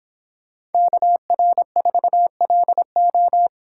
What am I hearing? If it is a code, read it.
Morse code, heard as a keyed tone, about 24 wpm, 700 Hz. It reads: KR4LO